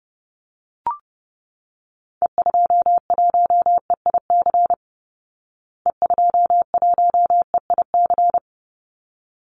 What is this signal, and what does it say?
Morse code, heard as a keyed tone, about 30 wpm, 700 Hz. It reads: E21EIC E21EIC
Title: Call Signs Repeated 3x - 30wpm